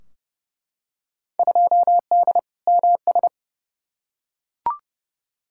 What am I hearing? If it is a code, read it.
Morse code, heard as a keyed tone, about 31 wpm, 700 Hz. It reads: 2D MS
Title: Field Day Exchanges Encore - 30wpm